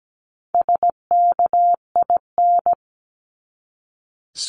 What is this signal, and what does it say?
Morse code, heard as a keyed tone, about 17 wpm, 700 Hz. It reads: SKIN
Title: Top 1001-1250 Words 17wpm